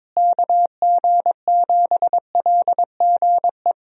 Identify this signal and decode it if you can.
Morse code, heard as a keyed tone, about 22 wpm, 700 Hz. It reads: KG7LGE
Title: USA Call Signs - 6 Characters - Repeated 3x - 22wpm